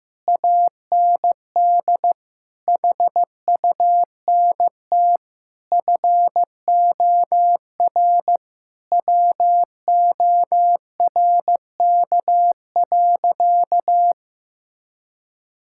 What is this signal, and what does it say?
Morse code, heard as a keyed tone, about 16 wpm, 700 Hz. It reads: AND HUNT FOR WORK.
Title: Sentences from Top 1600 Words 15wpm